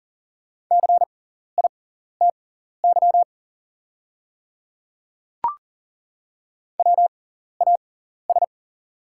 Morse code, heard as a keyed tone, about 40 wpm, spaced as for 14 wpm, 690 Hz. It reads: CITY WAS
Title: 6x Character Spacing - Top 300 Words - 40wpm